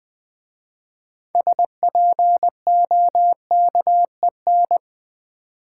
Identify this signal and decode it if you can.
Morse code, heard as a keyed tone, about 20 wpm, 700 Hz. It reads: SPOKEN
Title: 6 Letter Words - First 3 Letters Given - 20wpm